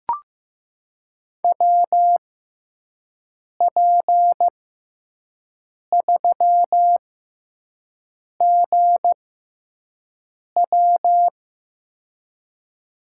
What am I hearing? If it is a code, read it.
Morse code, heard as a keyed tone, about 15 wpm, spaced as for 5 wpm, 700 Hz. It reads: WP3GW
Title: USA Call Signs - 5 Characters - 6x Character Spacing - 15wpm